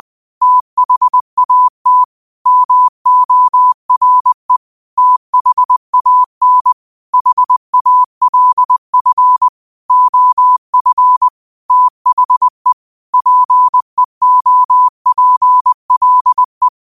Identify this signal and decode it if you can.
Morse code, heard as a keyed tone, about 20 wpm, 1000 Hz. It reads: THAT MORE THAN HALF OF THE PEOPLE
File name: qotdfiles / 20wpm